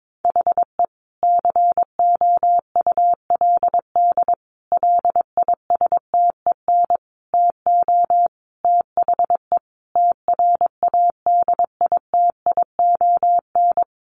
Morse code, heard as a keyed tone, about 22 wpm, 700 Hz. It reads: HE COULD LISTEN TO THE TRADITION
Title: Sentences from Top 1500 Words 22wpm